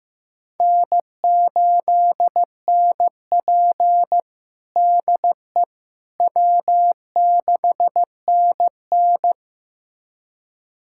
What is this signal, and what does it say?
Morse code, heard as a keyed tone, about 15 wpm, 700 Hz. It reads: N8NP DE W6NN
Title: USA Callsign DE Callsign - 4 Characters - 15wpm